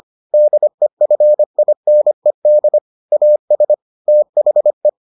Morse code, heard as a keyed tone, about 25 wpm, 600 Hz. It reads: DEFINED AS THE